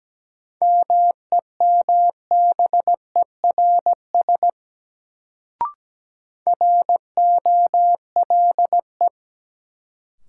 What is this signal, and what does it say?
Morse code, heard as a keyed tone, about 17 wpm, 700 Hz. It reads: MEMBERS ROLE
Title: Sets of 3 Words - New Word Drill - 17wpm